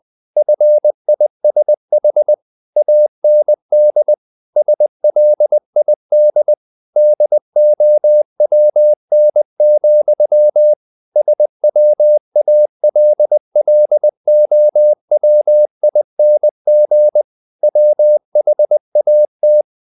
Morse code, heard as a keyed tone, about 20 wpm, 600 Hz. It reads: FISH AND SLID DOWN, SWALLOWING WHAT
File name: the old man and the sea part 2 20wpm